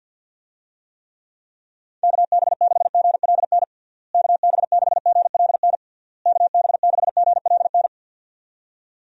Morse code, heard as a keyed tone, about 52 wpm, 700 Hz. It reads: KB6GLN KB6GLN KB6GLN
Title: USA Call Signs - 6 Characters - Repeated 3x - 50wpm